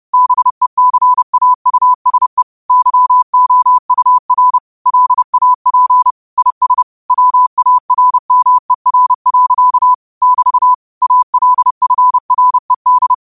Morse code, heard as a keyed tone, about 30 wpm, 1000 Hz. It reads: BECAUSE YOUR LAP IS WARMER. <BT> ALFRED